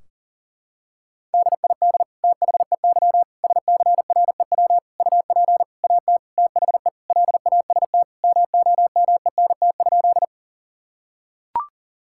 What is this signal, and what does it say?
Morse code, heard as a keyed tone, about 40 wpm, 700 Hz. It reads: DID THEY SCREW UP AT THE LAST MOMENT?
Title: Sentences from Top 1600 Words 40wpm